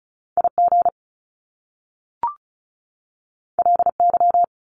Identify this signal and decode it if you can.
Morse code, heard as a keyed tone, about 36 wpm, 700 Hz. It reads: IG LY
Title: Top Two Letter Combinations in Words 35wpm